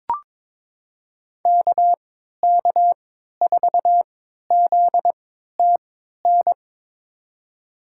Morse code, heard as a keyed tone, about 23 wpm, spaced as for 13 wpm, 700 Hz. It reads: KK4ZTN
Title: USA Call Signs - 6 Characters - 3x Character Spacing - 22wpm